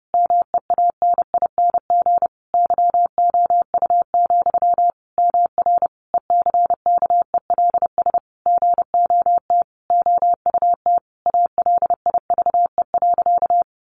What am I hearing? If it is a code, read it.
Morse code, heard as a keyed tone, about 30 wpm, 700 Hz. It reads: MEANING YOU, MR ECKELS GOT OUT ALIVE.
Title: A Sound Of Thunder 30wpm